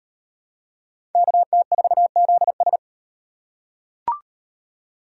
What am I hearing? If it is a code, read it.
Morse code, heard as a keyed tone, about 34 wpm, 700 Hz. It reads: KT4ZS